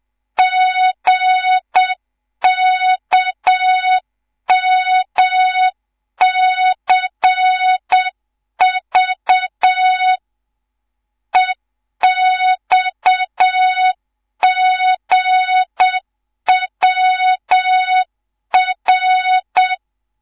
Morse code, heard as a keyed tone, about 7 wpm, 3700 Hz. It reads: GKMCV EXGWR
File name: Morze-HA2MN / hang / 11 GYAKORLAT